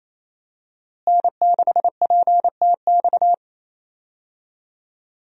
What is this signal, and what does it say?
Morse code, heard as a keyed tone, about 28 wpm, 700 Hz. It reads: N6PTX
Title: USA Call Signs - 5 Characters - Repeated 3x - 28wpm